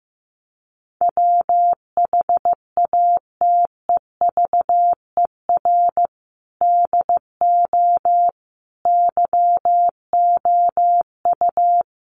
Morse code, heard as a keyed tone, about 15 wpm, 700 Hz. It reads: WHATEVER DO YOU
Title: Sentences from Top 1100 Words 15wpm